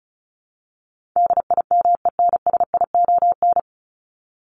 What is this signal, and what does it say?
Morse code, heard as a keyed tone, about 34 wpm, 700 Hz. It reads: DIMENSION
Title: Top 1501-1750 Words 35wpm